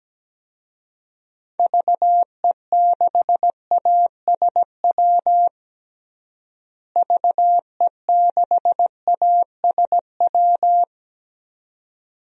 Morse code, heard as a keyed tone, about 17 wpm, 700 Hz. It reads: VE6ASW VE6ASW
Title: DXCC-100 Call Signs - 6 Characters - Repeated 2x - 17wpm